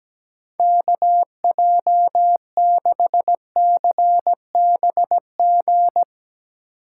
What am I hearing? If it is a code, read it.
Morse code, heard as a keyed tone, about 17 wpm, 700 Hz. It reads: KJ6CBG
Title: USA Call Signs - 6 Characters - 3x Character Spacing - 17wpm